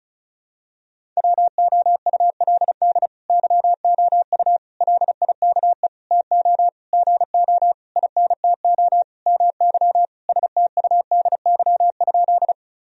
Morse code, heard as a keyed tone, about 36 wpm, 700 Hz. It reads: WOULD YOU LIKE TO GO INTO MY STUDY?
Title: Sentences from Top 1000 Words - Review Part 1 - 35wpm